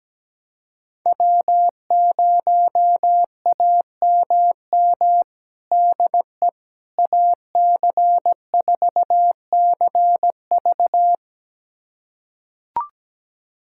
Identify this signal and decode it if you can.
Morse code, heard as a keyed tone, about 17 wpm, 700 Hz. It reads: W0AMM DE AC4CV
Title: USA Callsign DE Callsign 17wpm